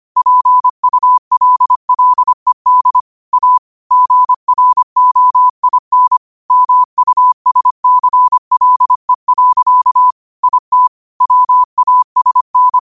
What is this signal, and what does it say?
Morse code, heard as a keyed tone, about 25 wpm, 1000 Hz. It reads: PULLED A GROIN MUSCLE. IT WASN
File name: qotdfiles / 25wpm